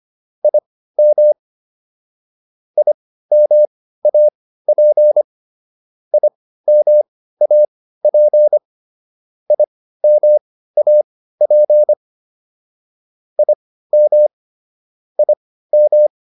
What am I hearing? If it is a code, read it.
Morse code, heard as a keyed tone, about 25 wpm, spaced as for 15 wpm, 600 Hz. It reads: IM IMAP IMAP IMAP IM IM
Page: intermediate-practice-files